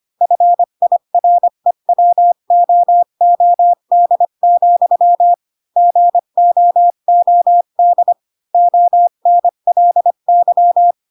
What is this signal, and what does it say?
Morse code, heard as a keyed tone, about 25 wpm, 700 Hz. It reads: FIREWOOD, GOOD ONLY
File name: Pinocchio / Pinocchio ch1 20 25 ik1hge